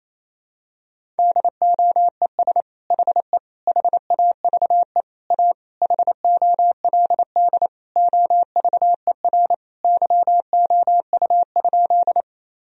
Morse code, heard as a keyed tone, about 27 wpm, 700 Hz. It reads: DOES HE HAVE A HOLD OVER YOU?
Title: Sentences from Top 1000 Words - Review Part 2 - 28wpm